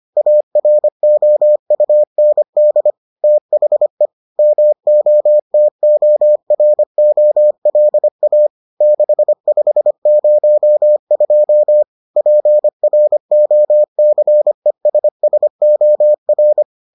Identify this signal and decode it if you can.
Morse code, heard as a keyed tone, about 25 wpm, 600 Hz. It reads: AROUND THE MOTOROLA 6502 PROCESSOR